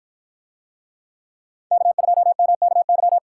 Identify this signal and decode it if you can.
Morse code, heard as a keyed tone, about 53 wpm, 700 Hz. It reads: K2MKY